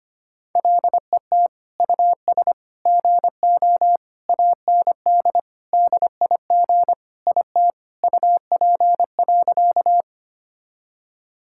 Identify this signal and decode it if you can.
Morse code, heard as a keyed tone, about 25 wpm, 700 Hz. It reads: LET US GO AND DIG IT UP.
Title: Sentences from Top 1600 Words 25wpm